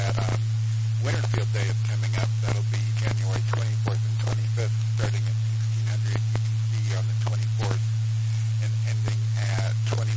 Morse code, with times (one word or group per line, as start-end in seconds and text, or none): none